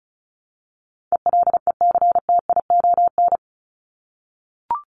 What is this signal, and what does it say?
Morse code, heard as a keyed tone, about 34 wpm, 700 Hz. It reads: ELECTION
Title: Top 1251-1500 Words 35wpm